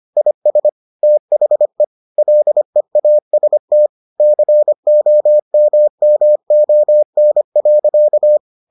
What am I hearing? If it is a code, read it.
Morse code, heard as a keyed tone, about 25 wpm, 600 Hz. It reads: IS THE LEAST COMMON.